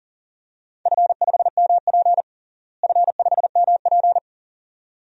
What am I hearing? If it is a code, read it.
Morse code, heard as a keyed tone, about 40 wpm, 700 Hz. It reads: F5MP F5MP
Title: DXCC-100 Call Signs - 4 Characters - Repeated 2x - 40wpm